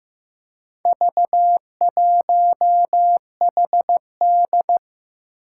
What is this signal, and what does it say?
Morse code, heard as a keyed tone, about 15 wpm, 700 Hz. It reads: V1HD
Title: Four Challenging Letter-Number Combinations 15wpm